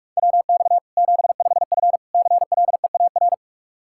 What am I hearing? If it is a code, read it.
Morse code, heard as a keyed tone, about 44 wpm, 700 Hz. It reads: WX 75F CLEAR
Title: K1USN SST QSO Practice 45wpm